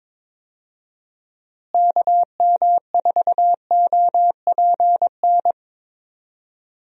Morse code, heard as a keyed tone, about 22 wpm, 700 Hz. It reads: KM4OPN
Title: USA Call Signs - 6 Characters - Repeated 2x - 22wpm